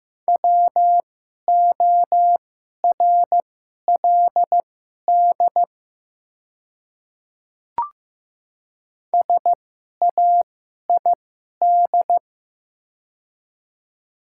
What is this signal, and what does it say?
Morse code, heard as a keyed tone, about 15 wpm, spaced as for 11 wpm, 700 Hz. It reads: WORLD SAID
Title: 2x Character Spacing - Top 300 Words - 15wpm